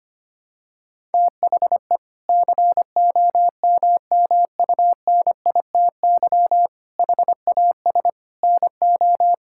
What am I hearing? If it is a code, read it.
Morse code, heard as a keyed tone, about 25 wpm, 700 Hz. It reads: THE COMMUNITY HAS NO
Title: Sentences from Top 1000 Words - Review Part 5 - 25wpm